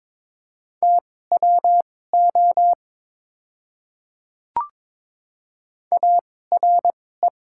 Morse code, heard as a keyed tone, about 22 wpm, spaced as for 16 wpm, 690 Hz. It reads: TWO ARE
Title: 2x Character Spacing - Top 200 Words - 22wpm